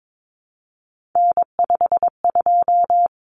Morse code, heard as a keyed tone, about 22 wpm, 700 Hz. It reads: N52
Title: Common Cut Numbers - 3-Digit 22wpm